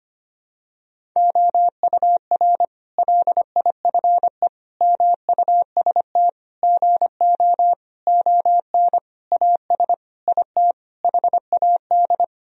Morse code, heard as a keyed tone, about 25 wpm, 700 Hz. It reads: OUR LIFE MUST GO ON AS IT HAD
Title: Sentences from Top 1000 Words - Review Part 3 - 25wpm